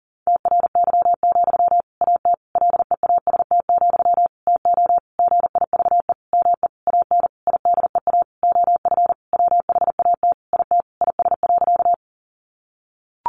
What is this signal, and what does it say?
Morse code, heard as a keyed tone, about 40 wpm, 700 Hz. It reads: TRY, AT LEAST, TO GIVE ME AN IDEA OF WHAT IT IS.
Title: Sentences from Top 700 Words 40wpm